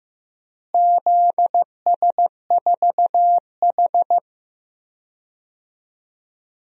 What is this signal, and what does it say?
Morse code, heard as a keyed tone, about 15 wpm, 700 Hz. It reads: ZS4H
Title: DXCC-100 Call Signs - 4 Characters - Repeated 3x - 15wpm